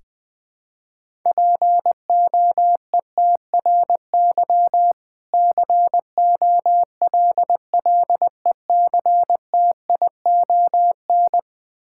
Morse code, heard as a keyed tone, about 20 wpm, 700 Hz. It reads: POETRY COLLECTION
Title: Difficult Words with Hints - 20wpm